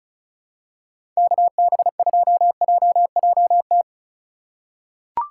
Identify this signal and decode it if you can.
Morse code, heard as a keyed tone, about 36 wpm, 700 Hz. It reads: KB2JJT
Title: USA Call Signs - 6 Characters - 4x Character Spacing - 35wpm